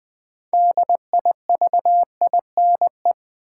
Morse code, heard as a keyed tone, about 20 wpm, 700 Hz. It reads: DIVINE
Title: Build A Word - 2 Letters At A Time - 6-Letter Words - 20wpm